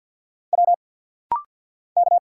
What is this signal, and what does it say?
Morse code, heard as a keyed tone, about 49 wpm, 700 Hz. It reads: W K
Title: Single Letter-Number - Rapid-Fire 50wpm